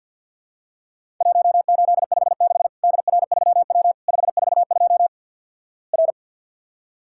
Morse code, heard as a keyed tone, about 50 wpm, 700 Hz. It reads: 1856 DR3W 542 R
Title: Worked All Europe DX Contest - QTC Exchange - 50wpm